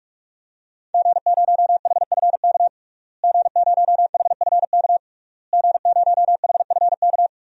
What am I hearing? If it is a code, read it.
Morse code, heard as a keyed tone, about 46 wpm, 700 Hz. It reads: G0HFK G0HFK G0HFK
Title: DXCC-100 Call Signs - 5 Characters - Repeated 3x - 45wpm